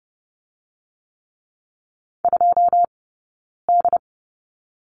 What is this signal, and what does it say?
Morse code, heard as a keyed tone, about 30 wpm, 700 Hz. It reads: 2 D
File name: Lesson 18d - Letter m - Two Characters 30wpm